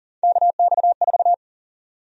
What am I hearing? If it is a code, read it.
Morse code, heard as a keyed tone, about 40 wpm, 700 Hz. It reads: KX4
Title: USA Call Sign Prefixes - Repeated 2x - 40wpm